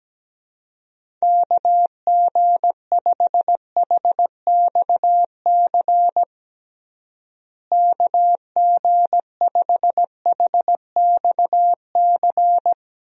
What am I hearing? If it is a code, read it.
Morse code, heard as a keyed tone, about 17 wpm, 700 Hz. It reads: KG5HXC KG5HXC
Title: USA Call Signs - 6 Characters - Repeated 3x - 17wpm